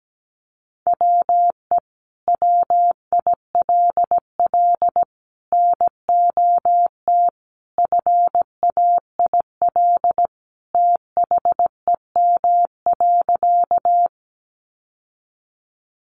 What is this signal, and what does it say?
Morse code, heard as a keyed tone, about 17 wpm, 700 Hz. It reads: WE WILL NOT FAIL THEM.
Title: Sentences from Top 1100 Words 17wpm